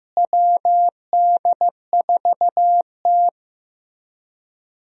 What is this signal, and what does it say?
Morse code, heard as a keyed tone, about 15 wpm, 700 Hz. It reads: WD4T
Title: USA Call Signs - 4 Characters - Repeated 3x - 15wpm